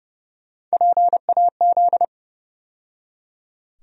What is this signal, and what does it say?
Morse code, heard as a keyed tone, about 30 wpm, 700 Hz. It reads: PAZ
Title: USA Call Sign Postfixes - 3 Letters 30wpm